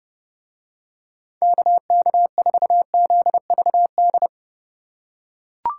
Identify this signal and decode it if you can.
Morse code, heard as a keyed tone, about 30 wpm, 700 Hz. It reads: KK4ZVD